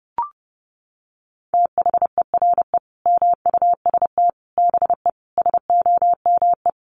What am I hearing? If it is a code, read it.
Morse code, heard as a keyed tone, about 30 wpm, 700 Hz. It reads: THERE MUST BE SOME
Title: Sentences from Top 700 Words 30wpm